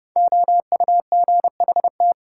Morse code, heard as a keyed tone, about 30 wpm, 700 Hz. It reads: OUGHT
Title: Sets of 5 Words Encore - New Word Drill - 30wpm